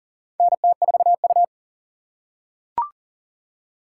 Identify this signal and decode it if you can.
Morse code, heard as a keyed tone, about 40 wpm, 700 Hz. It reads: NT4U
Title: USA Call Signs - 4 Characters - 6x Character Spacing - 40wpm